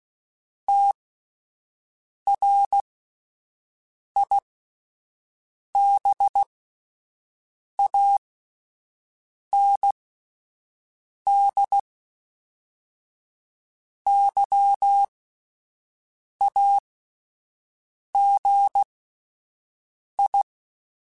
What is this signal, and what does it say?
Morse code, heard as a keyed tone, about 16 wpm, spaced as for 5 wpm, 800 Hz. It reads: TRIBAND YAGI
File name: fromSMs / FromSMs mp3s / Antennas@5